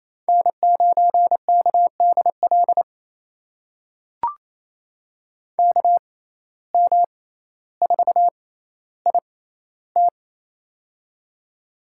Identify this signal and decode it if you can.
Morse code, heard as a keyed tone, about 28 wpm, spaced as for 14 wpm, 700 Hz. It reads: N9KDL KM4IT
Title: USA Call Signs - 5 Characters - 6x Character Spacing - 28wpm